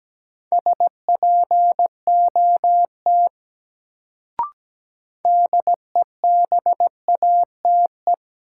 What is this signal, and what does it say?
Morse code, heard as a keyed tone, about 17 wpm, 700 Hz. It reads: SPOT DEBATE
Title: Sentences from Top 1100 Words - New Word Drill - 17wpm